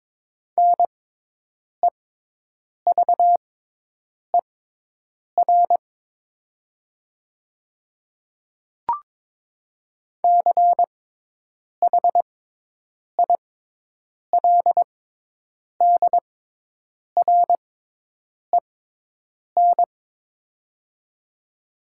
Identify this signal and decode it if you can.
Morse code, heard as a keyed tone, about 21 wpm, spaced as for 8 wpm, 700 Hz. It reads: NEVER CHILDREN
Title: 6x Character Spacing - Top 500 Words - 22wpm